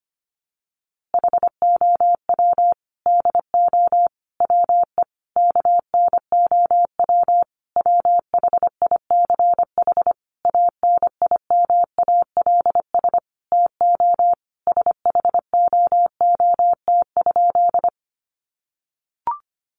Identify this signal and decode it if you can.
Morse code, heard as a keyed tone, about 25 wpm, 700 Hz. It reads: HOW DO WE KNOW WHICH ANIMALS TO SHOOT?
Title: A Sound Of Thunder 25wpm